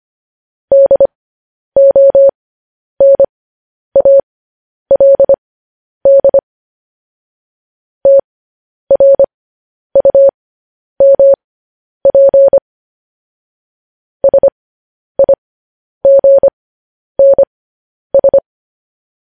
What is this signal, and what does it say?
Morse code, heard as a keyed tone, about 25 wpm, spaced as for 10 wpm, 550 Hz. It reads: DONALD TRUMP SIGNS
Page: news headlines in morse code at 10 wpm